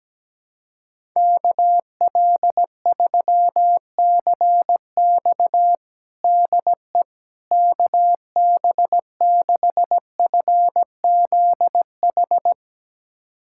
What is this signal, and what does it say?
Morse code, heard as a keyed tone, about 17 wpm, 700 Hz. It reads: KL3CX DE KB6FZH
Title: USA Callsign DE Callsign 17wpm